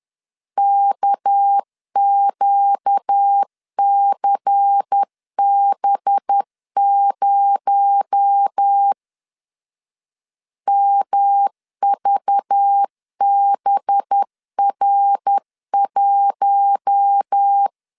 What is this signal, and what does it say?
Morse code, heard as a keyed tone, about 10 wpm, 800 Hz. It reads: KQCB0 MVBR1
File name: CW / DET / 1 6 1500C S B 10wpm 4137